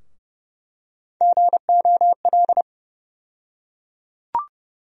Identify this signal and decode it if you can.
Morse code, heard as a keyed tone, about 30 wpm, 700 Hz. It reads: GOL
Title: USA Call Sign Postfixes - 3 Letters 30wpm